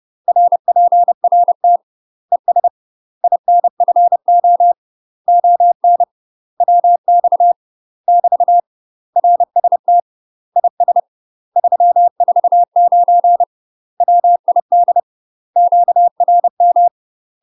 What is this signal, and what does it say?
Morse code, heard as a keyed tone, about 30 wpm, 700 Hz. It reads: RPRT ES INFO ON WX <BT> RST IS 349 WID QRM
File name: UK MASTER QSOs TRAINING / 30WPM / UK MASTER QSOs 0009